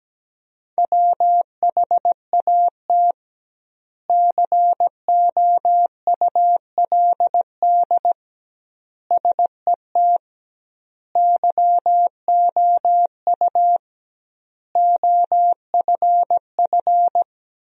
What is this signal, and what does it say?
Morse code, heard as a keyed tone, about 17 wpm, 700 Hz. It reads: WHAT COULD SET YOU OFF
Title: 2x Word-Spacing - Sentences from Top 200 Words - 17wpm